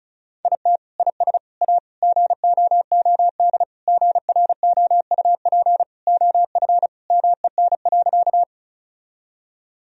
Morse code, heard as a keyed tone, about 34 wpm, 700 Hz. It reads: IT IS A GOOD GROUP OF MEN.
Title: Sentences from Top 1000 Words - Review Part 1 - 35wpm